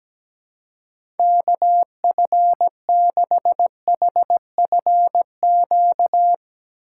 Morse code, heard as a keyed tone, about 17 wpm, 700 Hz. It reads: KF6HFQ
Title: USA Call Signs - 6 Characters - 2x Character Spacing - 17wpm